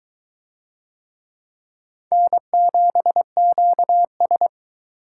Morse code, heard as a keyed tone, about 22 wpm, 700 Hz. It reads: N7QS